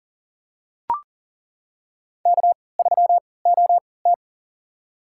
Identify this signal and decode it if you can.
Morse code, heard as a keyed tone, about 40 wpm, spaced as for 23 wpm, 700 Hz. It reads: K3OT
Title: USA Call Signs - 4 Characters - 3x Character Spacing - 40wpm